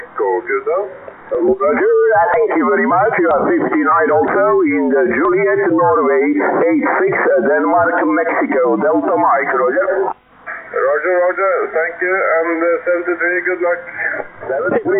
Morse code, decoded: none